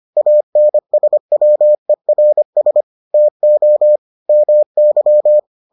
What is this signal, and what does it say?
Morse code, heard as a keyed tone, about 25 wpm, 600 Hz. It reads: ANSWERS TO MY